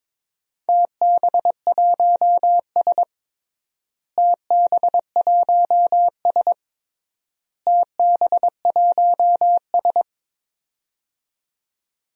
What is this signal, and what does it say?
Morse code, heard as a keyed tone, about 22 wpm, 700 Hz. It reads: TB1S TB1S TB1S
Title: DXCC-100 Call Signs - 4 Characters - Repeated 3x - 22wpm